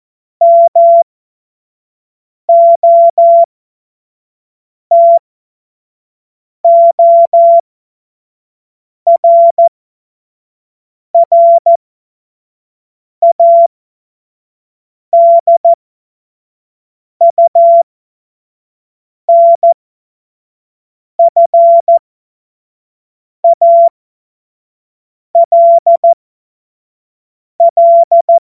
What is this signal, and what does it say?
Morse code, heard as a keyed tone, about 14 wpm, spaced as for 5 wpm, 680 Hz. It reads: MOTORRADUNFALL
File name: Morsekurs / Witze / Witz 04. Juli 2023 25 F70